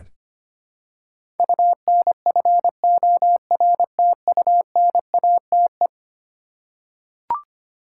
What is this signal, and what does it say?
Morse code, heard as a keyed tone, about 25 wpm, 700 Hz. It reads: UNFORTUNATE
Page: morse-code-ninja-practice-25wpm